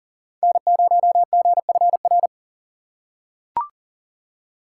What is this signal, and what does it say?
Morse code, heard as a keyed tone, about 40 wpm, 700 Hz. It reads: N0GFR
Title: USA Call Signs - 5 Characters - 3x Character Spacing - 40wpm